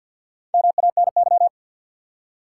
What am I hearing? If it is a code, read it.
Morse code, heard as a keyed tone, about 64 wpm, 700 Hz. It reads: MANY